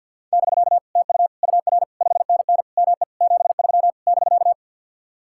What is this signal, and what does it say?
Morse code, heard as a keyed tone, about 61 wpm, 700 Hz. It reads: <BK> TU UR 5NN ME 73 <BK>
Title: Speed-Racing Common QSO Phrases Encore - 50wpm